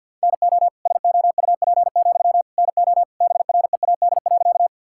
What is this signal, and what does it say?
Morse code, heard as a keyed tone, about 50 wpm, 700 Hz. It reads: NO SOUP, NO BREAD.
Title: Sentences from Top 1600 Words 50wpm